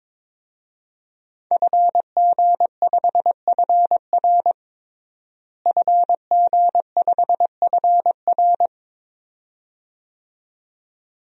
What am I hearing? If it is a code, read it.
Morse code, heard as a keyed tone, about 22 wpm, 700 Hz. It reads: FG5FR FG5FR
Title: DXCC-100 Call Signs - 5 Characters - Repeated 2x - 22wpm